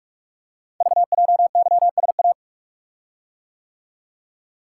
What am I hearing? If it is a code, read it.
Morse code, heard as a keyed tone, about 44 wpm, 700 Hz. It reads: VJYSA
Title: Code Groups - 5 Letters - 45wpm